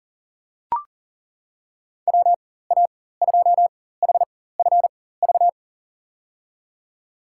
Morse code, heard as a keyed tone, about 40 wpm, spaced as for 19 wpm, 700 Hz. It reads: WA2HFV
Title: USA Call Signs - 6 Characters - 4x Character Spacing - 40wpm